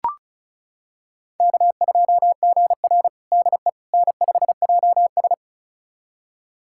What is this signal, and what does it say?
Morse code, heard as a keyed tone, about 36 wpm, 700 Hz. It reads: K2GR DE N5JS
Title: USA Callsign DE Callsign - 4 Characters - 35wpm